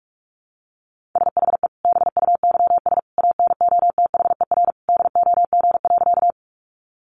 Morse code, heard as a keyed tone, about 44 wpm, 700 Hz. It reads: SHE BUYS ANOTHER DOG.
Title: Easy Phrases 45wpm